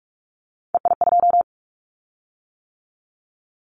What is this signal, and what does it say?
Morse code, heard as a keyed tone, about 46 wpm, 700 Hz. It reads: EI2
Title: Callsign Prefixes for Top 50 DXCC - 45wpm